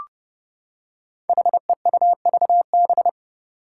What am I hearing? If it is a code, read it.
Morse code, heard as a keyed tone, about 30 wpm, 700 Hz. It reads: HEUVB
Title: Five Challenging Letter-Number Combinations 30wpm